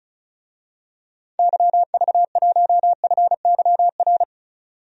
Morse code, heard as a keyed tone, about 36 wpm, 700 Hz. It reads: YV1FYR
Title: DXCC-100 Call Signs - 6 Characters - 35wpm